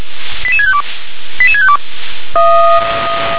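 Five-tone selective call (ZVEI), heard as quick runs of five tones, tones of about 70 ms. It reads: 88652 88652
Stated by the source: Five-tone ZVEI selective call received by SDR.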